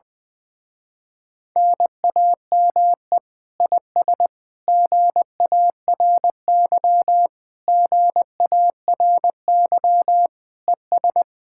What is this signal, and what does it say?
Morse code, heard as a keyed tone, about 20 wpm, 700 Hz. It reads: NAME IS GARY GARY ES
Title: Speed-Racing Common QSO Phrases - 20wpm